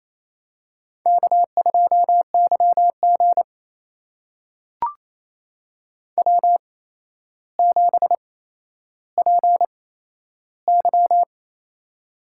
Morse code, heard as a keyed tone, about 29 wpm, spaced as for 12 wpm, 700 Hz. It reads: K2YG W7PY
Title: USA Call Signs - 4 Characters - 8x Character Spacing - 28wpm